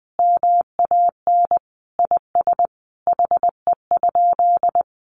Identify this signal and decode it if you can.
Morse code, heard as a keyed tone, about 19 wpm, 700 Hz. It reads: MAN IS HE?
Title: Sentences from Top 1300 Words 20wpm